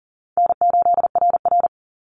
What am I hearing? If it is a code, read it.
Morse code, heard as a keyed tone, about 40 wpm, 700 Hz. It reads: N8RR
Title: Easy Call Signs Repeated 3x - 40wpm